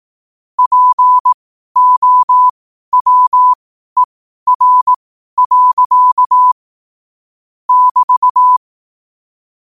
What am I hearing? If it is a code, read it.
Morse code, heard as a keyed tone, about 18 wpm, spaced as for 12 wpm, 1000 Hz. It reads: POWER. <BT>
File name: qotdfiles / 12wpm